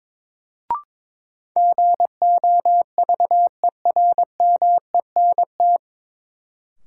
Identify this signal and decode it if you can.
Morse code, heard as a keyed tone, about 22 wpm, 700 Hz. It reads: GOVERMENT
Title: Sentences from Top 1300 Words - New Word Drill - 22wpm